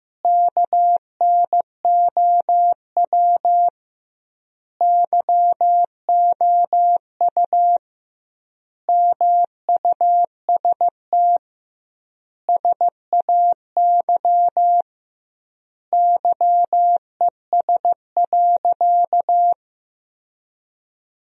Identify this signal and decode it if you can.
Morse code, heard as a keyed tone, about 15 wpm, 700 Hz. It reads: KNOW YOU MUST SAY YES.
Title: 2x Word-Spacing - Sentences from Top 500 Words - 15wpm